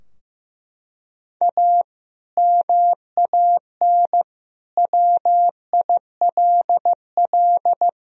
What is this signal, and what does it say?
Morse code, heard as a keyed tone, about 15 wpm, 700 Hz. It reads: A MAN WILL
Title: Sentences from Top 1000 Words - Review Part 2 - 15wpm